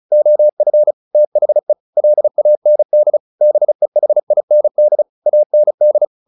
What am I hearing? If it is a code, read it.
Morse code, heard as a keyed tone, about 34 wpm, 600 Hz. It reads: OF THE LAND BEHIND AND
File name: the old man and the sea / the old man and the sea 1 35wpm